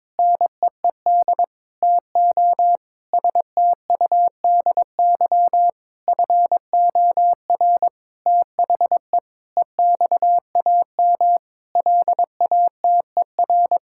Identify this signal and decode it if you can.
Morse code, heard as a keyed tone, about 22 wpm, 700 Hz. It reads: NEED TO STUDY FOR THE EXAM LATER
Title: Sentences from Top 1000 Words - Review Part 5 - 22wpm